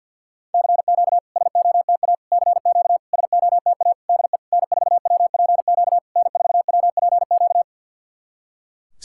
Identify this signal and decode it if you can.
Morse code, heard as a keyed tone, about 50 wpm, 700 Hz. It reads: CQ SOTA CQ SOTA DE N4WPQ N4WPQ
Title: SOTA QSO Practice 50wpm